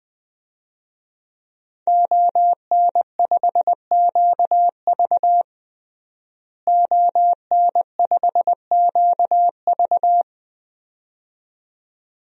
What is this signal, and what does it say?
Morse code, heard as a keyed tone, about 20 wpm, 700 Hz. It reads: ON5QV ON5QV
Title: DXCC-100 Call Signs - 5 Characters - Repeated 2x - 20wpm